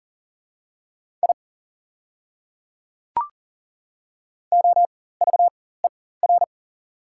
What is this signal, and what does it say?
Morse code, heard as a keyed tone, about 37 wpm, spaced as for 18 wpm, 690 Hz. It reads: I OVER
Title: 4x Character Spacing - Top 200 Words - 40wpm